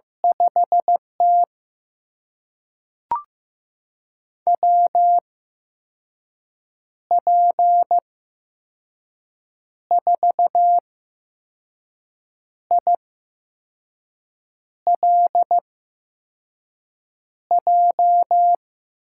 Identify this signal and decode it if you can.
Morse code, heard as a keyed tone, about 15 wpm, spaced as for 5 wpm, 700 Hz. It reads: 5T WP4ILJ
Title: USA Call Signs - 8x Character Spacing - 15wpm